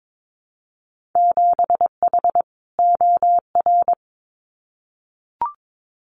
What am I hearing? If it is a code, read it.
Morse code, heard as a keyed tone, about 22 wpm, 700 Hz. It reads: 7H OR
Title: Winter Field Day Exchanges Encore 22wpm